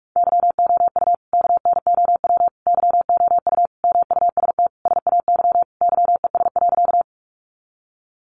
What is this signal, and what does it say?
Morse code, heard as a keyed tone, about 44 wpm, 700 Hz. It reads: YOU KNOW YOU MUST SAY YES.
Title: Sentences from Top 500 Words 45wpm